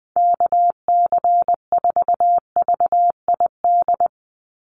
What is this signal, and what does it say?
Morse code, heard as a keyed tone, about 20 wpm, 700 Hz. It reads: KC4VID
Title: USA Call Signs 20wpm